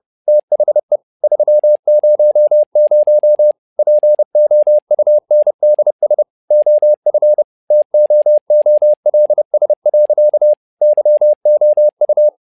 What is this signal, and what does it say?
Morse code, heard as a keyed tone, about 30 wpm, 600 Hz. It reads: THE 300 POUNDS OF TOOLS. YOU
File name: learning center / tower trouble / tower trouble 30wpm